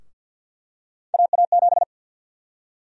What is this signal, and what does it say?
Morse code, heard as a keyed tone, about 52 wpm, 700 Hz. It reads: AA7
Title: USA Call Sign Prefixes - Repeated 2x - 50wpm